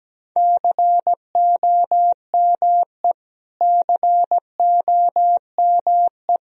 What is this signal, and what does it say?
Morse code, heard as a keyed tone, about 17 wpm, 700 Hz. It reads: COME COME
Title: Top 100 Words Repeated 17wpm